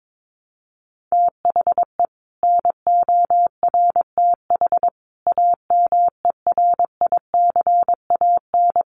Morse code, heard as a keyed tone, about 22 wpm, 700 Hz. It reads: THE NORTH AMERICAN
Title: Sets of 5 Words Encore 22wpm